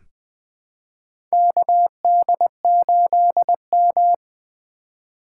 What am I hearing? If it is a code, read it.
Morse code, heard as a keyed tone, about 20 wpm, 700 Hz. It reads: KD8M
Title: USA Call Signs - 4 Characters - 3x Character Spacing - 20wpm